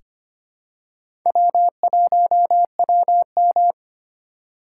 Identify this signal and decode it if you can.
Morse code, heard as a keyed tone, about 25 wpm, 700 Hz. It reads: W1WM
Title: USA Call Signs - 4 Characters - 4x Character Spacing - 25wpm